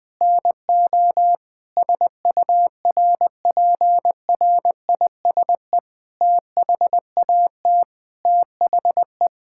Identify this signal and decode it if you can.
Morse code, heard as a keyed tone, about 20 wpm, 700 Hz. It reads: NO SURPRISE THAT THE
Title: Sentences with 8-Word Phrases - 20wpm